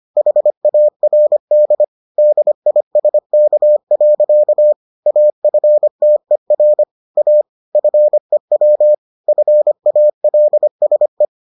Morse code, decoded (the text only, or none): HARD DISK. AFTER A FEW FALSE